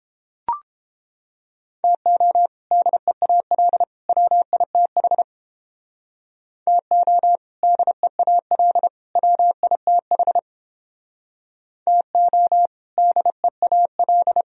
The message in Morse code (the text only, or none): TO DEAL WITH TO DEAL WITH TO DEAL